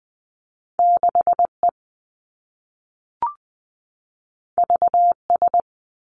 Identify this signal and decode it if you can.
Morse code, heard as a keyed tone, about 20 wpm, 700 Hz. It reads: 6 E V S
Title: Two Challenging Letter-Number Combinations 20wpm